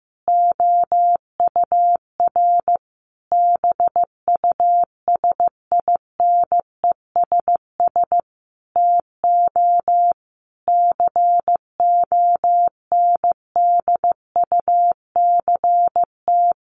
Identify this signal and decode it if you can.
Morse code, heard as a keyed tone, about 15 wpm, 700 Hz. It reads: OUR BUSINESS TO CONDUCT
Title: A Sound Of Thunder 15wpm